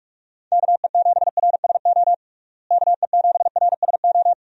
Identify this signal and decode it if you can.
Morse code, heard as a keyed tone, about 47 wpm, 700 Hz. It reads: KE7RSO KE7RSO
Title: USA Call Signs - 6 Characters - Repeated 3x - 45wpm